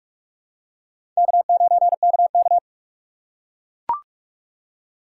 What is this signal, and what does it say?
Morse code, heard as a keyed tone, about 40 wpm, 700 Hz. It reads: K9KK